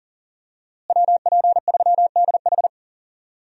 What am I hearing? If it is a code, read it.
Morse code, heard as a keyed tone, about 40 wpm, 700 Hz. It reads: WP3DH
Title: USA Call Signs - 5 Characters - Repeated 2x - 40wpm